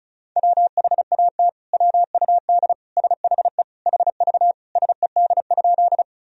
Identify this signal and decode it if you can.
Morse code, heard as a keyed tone, about 34 wpm, 700 Hz. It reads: WHAT WUD SHE HV SED?
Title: Sentences from Top 100 Words and Abbreviations 35wpm